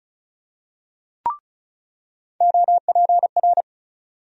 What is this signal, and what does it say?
Morse code, heard as a keyed tone, about 35 wpm, 700 Hz. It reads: OPR
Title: USA Call Sign Postfixes - 3 Letters 35wpm